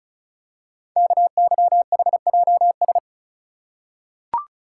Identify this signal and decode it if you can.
Morse code, heard as a keyed tone, about 36 wpm, 700 Hz. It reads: KYHJS
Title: Code Groups - 5 Letters - 35wpm